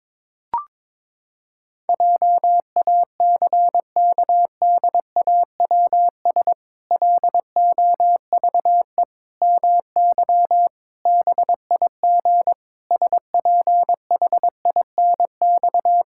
Morse code, heard as a keyed tone, about 22 wpm, 700 Hz. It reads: JACKDAWS LOVE MY BIG SPHINX
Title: Pangram Challenge v2 - 22wpm